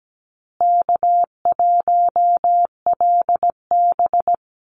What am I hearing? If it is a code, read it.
Morse code, heard as a keyed tone, about 17 wpm, 700 Hz. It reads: K1LB
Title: Easy Call Signs Repeated 2x - 17wpm